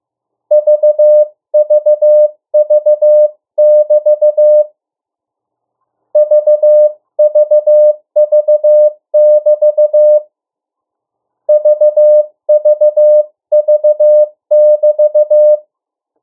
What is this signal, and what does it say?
Morse code, heard as a keyed tone, about 15 wpm, 600 Hz. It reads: VVV<BT> VVV<BT> VVV<BT>
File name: morse mp3 / 02 urok 1